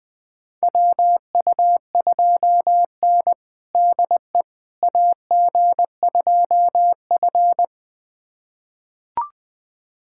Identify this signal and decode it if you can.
Morse code, heard as a keyed tone, about 20 wpm, 700 Hz. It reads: WU2N DE AG2F
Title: USA Callsign DE Callsign - 4 Characters - 20wpm